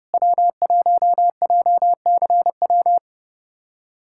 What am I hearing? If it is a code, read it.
Morse code, heard as a keyed tone, about 30 wpm, 700 Hz. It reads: W1JCW
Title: K1USN SST QSO Practice - 30wpm